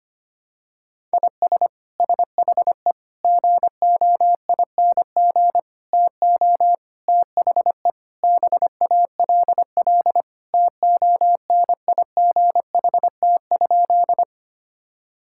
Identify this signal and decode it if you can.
Morse code, heard as a keyed tone, about 25 wpm, 700 Hz. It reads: IS SHE GOING TO THE BALL TONIGHT?
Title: Sentences from Top 1000 Words - Review Part 6 - 25wpm